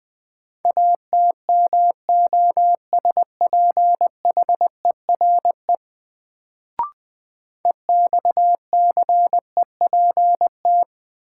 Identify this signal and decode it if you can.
Morse code, heard as a keyed tone, about 20 wpm, 700 Hz. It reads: ATMOSPHERE EXCEPT
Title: Sentences from Top 1600 Words - New Word Drill - 20wpm